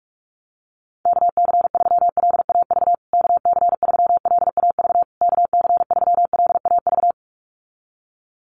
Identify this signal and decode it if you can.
Morse code, heard as a keyed tone, about 44 wpm, 700 Hz. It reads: KC3LAV KC3LAV KC3LAV
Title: USA Call Signs Repeated 3x 45wpm